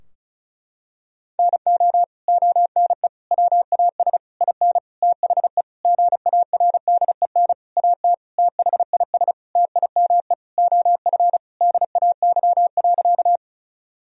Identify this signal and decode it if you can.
Morse code, heard as a keyed tone, about 36 wpm, 700 Hz. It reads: NO ONE WAS IN THE GARDEN AT THIS TIME OF DAY.
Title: Sentences from Top 1000 Words - Review Part 4 - 35wpm